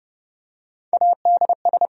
Morse code, heard as a keyed tone, about 28 wpm, 700 Hz. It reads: ADS